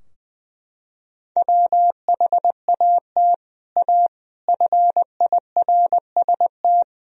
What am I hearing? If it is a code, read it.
Morse code, heard as a keyed tone, about 21 wpm, 700 Hz. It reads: WHAT A FIRST
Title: Sentences from Top 1000 Words - Review Part 2 - 20wpm